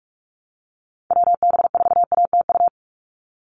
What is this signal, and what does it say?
Morse code, heard as a keyed tone, about 46 wpm, 690 Hz. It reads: WB4ATU
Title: USA Call Signs 45wpm